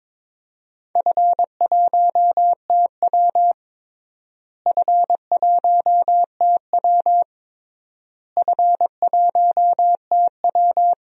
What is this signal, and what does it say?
Morse code, heard as a keyed tone, about 22 wpm, 700 Hz. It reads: F1TW F1TW F1TW
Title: DXCC-100 Call Signs - 4 Characters - Repeated 3x - 22wpm